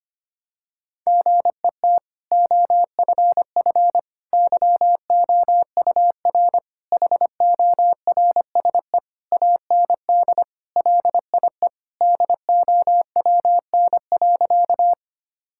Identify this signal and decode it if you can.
Morse code, heard as a keyed tone, about 25 wpm, 700 Hz. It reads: GET OFF YOUR HORSE AND LIE DOWN.
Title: Sentences from Top 1600 Words 25wpm